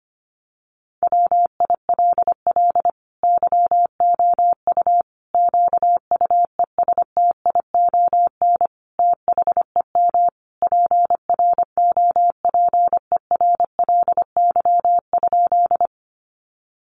Morse code, heard as a keyed tone, about 25 wpm, 700 Hz. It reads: WILL YOU QUESTION THEM PROPERLY?
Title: Sentences from Top 1100 Words 25wpm